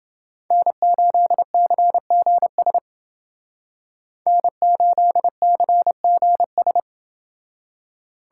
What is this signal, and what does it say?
Morse code, heard as a keyed tone, about 29 wpm, 700 Hz. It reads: N8CGS N8CGS